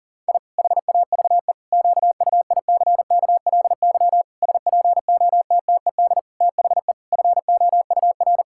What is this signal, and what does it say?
Morse code, heard as a keyed tone, about 40 wpm, 700 Hz. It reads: I HAVE QUICKLY SPOTTED THE FOUR
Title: Pangram Challenge v2 - 40wpm